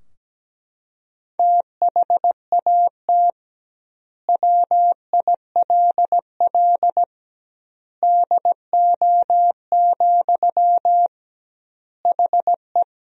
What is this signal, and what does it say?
Morse code, heard as a keyed tone, about 17 wpm, 700 Hz. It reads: THAT WILL DO, HE
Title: 2x Word-Spacing - Sentences from Top 100 Words - 17wpm